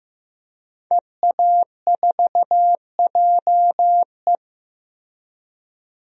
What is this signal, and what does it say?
Morse code, heard as a keyed tone, about 15 wpm, 700 Hz. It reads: EA4JE
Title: DXCC-100 Call Signs - 5 Characters - Repeated 2x - 15wpm